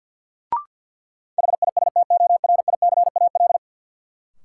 Morse code, heard as a keyed tone, about 52 wpm, 700 Hz. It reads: HISTORICAL
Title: Sentences from Top 600 Words - New Word Drill - 50wpm